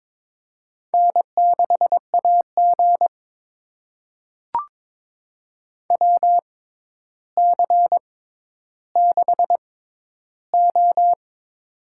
Morse code, heard as a keyed tone, about 22 wpm, spaced as for 11 wpm, 700 Hz. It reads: N6AG WC6O
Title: USA Call Signs - 4 Characters - 6x Character Spacing - 22wpm